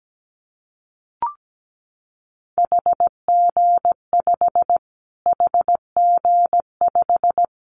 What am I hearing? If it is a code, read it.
Morse code, heard as a keyed tone, about 17 wpm, 700 Hz. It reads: HG5 HG5
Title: Callsign Prefixes for Top 25 DXCC - Repeated 3x - 17wpm